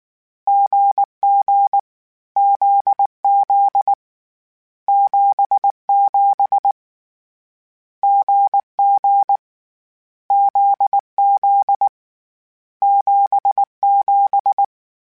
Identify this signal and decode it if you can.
Morse code, heard as a keyed tone, about 19 wpm, 800 Hz. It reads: GG ZZ 77 GG ZZ 77
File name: Difficult Combos / Difficult Combos-mp3s / GZ GZ7 @18